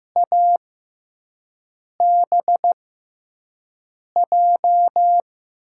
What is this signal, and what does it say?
Morse code, heard as a keyed tone, about 15 wpm, spaced as for 5 wpm, 700 Hz. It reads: ABJ
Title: USA Call Signs - 6 Characters - 6x Character Spacing - 15wpm